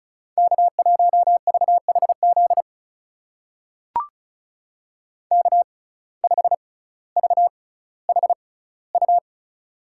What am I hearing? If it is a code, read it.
Morse code, heard as a keyed tone, about 36 wpm, spaced as for 18 wpm, 700 Hz. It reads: K1VHZ K5VHU
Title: USA Call Signs - 5 Characters - 6x Character Spacing - 35wpm